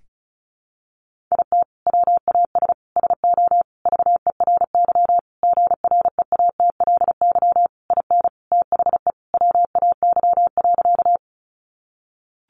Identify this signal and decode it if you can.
Morse code, heard as a keyed tone, about 34 wpm, 700 Hz. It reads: IT WAS SO VERY GREATLY IN THE WAY.
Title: Sentences from Top 1400 Words 35wpm